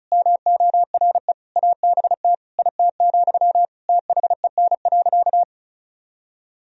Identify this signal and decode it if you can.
Morse code, heard as a keyed tone, about 36 wpm, 700 Hz. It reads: MORE ABT IT, THEN.
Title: Sentences from Top 200 Words and Abbreviations 35wpm